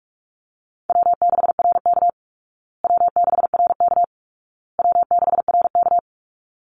Morse code, heard as a keyed tone, about 44 wpm, 700 Hz. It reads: W6RK W6RK W6RK
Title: Easy Call Signs Repeated 3x - 45wpm